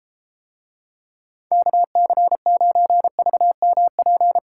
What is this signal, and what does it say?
Morse code, heard as a keyed tone, about 33 wpm, 700 Hz. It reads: KC9VMP